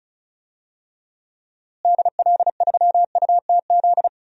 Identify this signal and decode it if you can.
Morse code, heard as a keyed tone, about 34 wpm, 700 Hz. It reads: DL3UTZ
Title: DXCC-100 Call Signs - 6 Characters - Repeated 3x - 35wpm